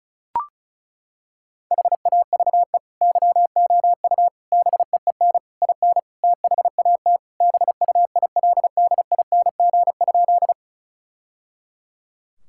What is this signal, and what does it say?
Morse code, heard as a keyed tone, about 34 wpm, 700 Hz. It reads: HAVE YOU BEEN IN THAT BUILDING?
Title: Sentences from Top 1000 Words - Review Part 5 - 35wpm